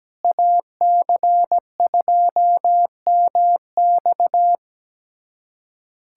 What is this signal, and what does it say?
Morse code, heard as a keyed tone, about 17 wpm, 700 Hz. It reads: AC2MX
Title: USA Callsign DE Callsign - 5 Characters - 17wpm